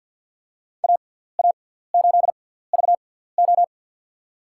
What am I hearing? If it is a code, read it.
Morse code, heard as a keyed tone, about 53 wpm, spaced as for 17 wpm, 700 Hz. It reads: AA8VO
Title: USA Call Signs - 5 Characters - 6x Character Spacing - 50wpm